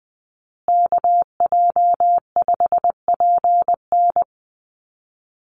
Morse code, heard as a keyed tone, about 20 wpm, 700 Hz. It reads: KJ5PN
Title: USA Call Signs Repeated 2x 20wpm